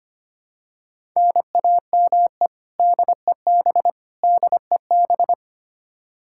Morse code, heard as a keyed tone, about 25 wpm, 700 Hz. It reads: NAME DEB DEB
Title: Practice QSO - Exchange 1 - Beginning - 25wpm